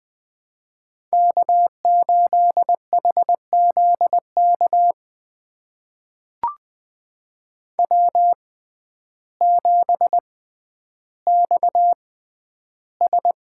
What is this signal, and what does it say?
Morse code, heard as a keyed tone, about 20 wpm, spaced as for 11 wpm, 700 Hz. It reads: K8HZK W7XS
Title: USA Call Signs - 6x Character Spacing - 20wpm